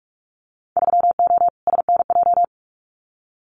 Morse code, heard as a keyed tone, about 44 wpm, 700 Hz. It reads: 3O SNJ
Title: Winter Field Day Exchanges Encore 45wpm